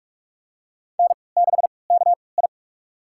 Morse code, heard as a keyed tone, about 41 wpm, spaced as for 24 wpm, 700 Hz. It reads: N6KI
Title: USA Call Signs - 3x Character Spacing - 45wpm